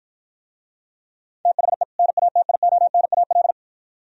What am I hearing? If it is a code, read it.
Morse code, heard as a keyed tone, about 53 wpm, 700 Hz. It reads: THE NATIONAL